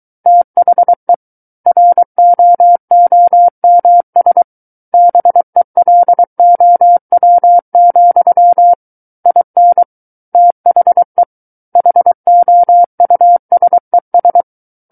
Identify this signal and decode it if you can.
Morse code, heard as a keyed tone, about 23 wpm, 700 Hz. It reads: THE ROOMS BELOW, IN THE HOUSES
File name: Book One / 14-Chapter 14 - 23 WPM